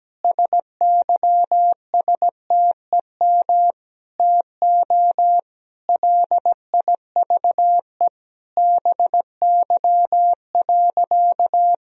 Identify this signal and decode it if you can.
Morse code, heard as a keyed tone, about 17 wpm, 700 Hz. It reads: SYSTEM TO LIVE BY.
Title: Sentences from Top 1000 Words - Review Part 3 - 17wpm